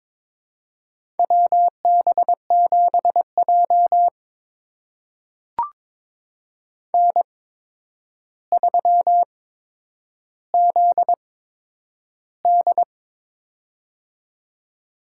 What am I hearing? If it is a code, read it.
Morse code, heard as a keyed tone, about 22 wpm, spaced as for 9 wpm, 700 Hz. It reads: WB7J N3ZD
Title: USA Call Signs - 4 Characters - 8x Character Spacing - 22wpm